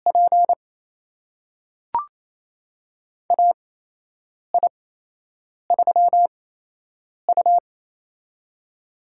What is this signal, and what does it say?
Morse code, heard as a keyed tone, about 27 wpm, spaced as for 8 wpm, 700 Hz. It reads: P AI3U
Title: USA Call Signs - 4 Characters - 8x Character Spacing - 28wpm